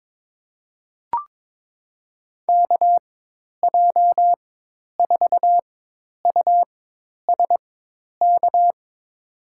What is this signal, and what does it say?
Morse code, heard as a keyed tone, about 21 wpm, spaced as for 10 wpm, 700 Hz. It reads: KJ4USK
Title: USA Call Signs - 6 Characters - 4x Character Spacing - 22wpm